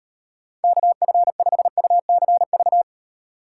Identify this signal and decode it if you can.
Morse code, heard as a keyed tone, about 38 wpm, 700 Hz. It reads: KF5UCV